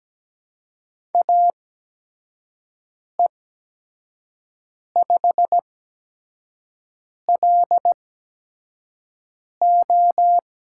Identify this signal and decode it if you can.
Morse code, heard as a keyed tone, about 17 wpm, spaced as for 5 wpm, 700 Hz. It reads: AE5LO
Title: USA Call Signs - 5 Characters - 8x Character Spacing - 17wpm